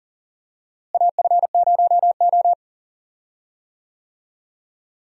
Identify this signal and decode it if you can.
Morse code, heard as a keyed tone, about 40 wpm, 700 Hz. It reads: AF0O
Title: USA Call Signs - 4 Characters - Repeated 3x - 40wpm